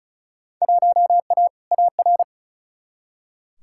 Morse code, heard as a keyed tone, about 36 wpm, 700 Hz. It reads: 1A AR
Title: Field Day Exchanges Encore - 35wpm